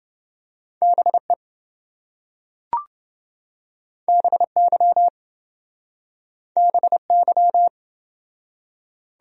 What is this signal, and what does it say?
Morse code, heard as a keyed tone, about 26 wpm, 700 Hz. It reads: BE BY BY